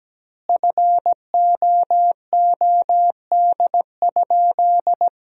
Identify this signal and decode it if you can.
Morse code, heard as a keyed tone, about 17 wpm, 700 Hz. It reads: FOOD?
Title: 2x Word-Spacing - Sentences from Top 300 Words - 17wpm